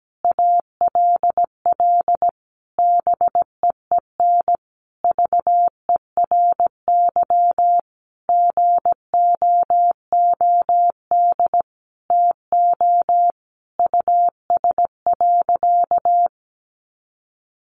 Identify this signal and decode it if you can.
Morse code, heard as a keyed tone, about 17 wpm, 700 Hz. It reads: ALL BEEN VERY GOOD TO US.
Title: Sentences from Top 200 Words 17wpm